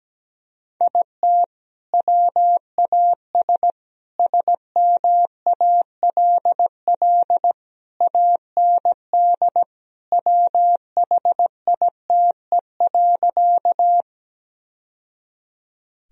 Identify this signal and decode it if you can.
Morse code, heard as a keyed tone, about 17 wpm, 700 Hz. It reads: IT WAS SMALL AND WHITE.
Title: Sentences from Top 1000 Words - Review Part 1 - 17wpm